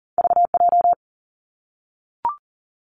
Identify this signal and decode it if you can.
Morse code, heard as a keyed tone, about 41 wpm, 700 Hz. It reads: V J
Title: Two Challenging Letter-Number Combinations 40wpm